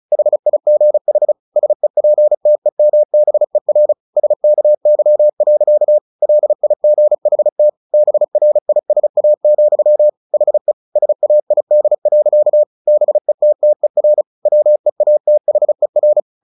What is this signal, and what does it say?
Morse code, heard as a keyed tone, about 34 wpm, 600 Hz. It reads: HIGH SEPTEMBER SKY. LIGHT BRISA, HE SAID. BETTER WEATHER
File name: the old man and the sea / the old man and the sea 1 35wpm